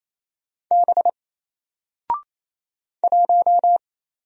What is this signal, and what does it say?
Morse code, heard as a keyed tone, about 27 wpm, 700 Hz. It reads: B 1
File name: Single-Letter-Number-28wpm